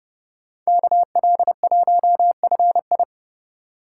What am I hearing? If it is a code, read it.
Morse code, heard as a keyed tone, about 34 wpm, 700 Hz. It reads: KL1FI
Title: Speed Racing - USA 2x2 Call Signs 30wpm